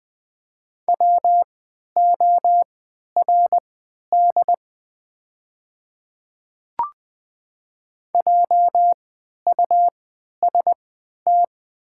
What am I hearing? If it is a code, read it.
Morse code, heard as a keyed tone, about 20 wpm, spaced as for 11 wpm, 700 Hz. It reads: WORD JUST
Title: 3x Character Spacing - Top 200 Words - 20wpm